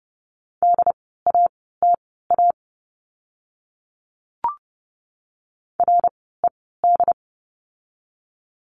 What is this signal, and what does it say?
Morse code, heard as a keyed tone, about 30 wpm, spaced as for 17 wpm, 700 Hz. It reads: DATA RED
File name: Lesson 17f - Letter u - Words 30wpm